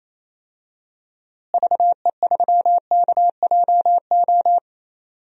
Morse code, heard as a keyed tone, about 28 wpm, 700 Hz. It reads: VE3KJO
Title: DXCC-100 Call Signs - 6 Characters - Repeated 2x - 28wpm